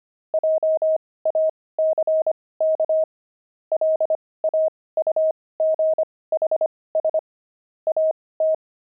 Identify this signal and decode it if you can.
Morse code, heard as a keyed tone, about 25 wpm, spaced as for 18 wpm, 630 Hz. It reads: JACK LAUGHS AT
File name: pangram-fs-18-cs-25-wived